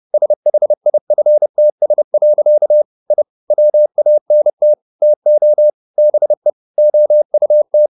Morse code, heard as a keyed tone, about 30 wpm, 600 Hz. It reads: SHIFTS. I WANT TO BE OUT
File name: the old man and the sea / the old man and the sea 1 30wpm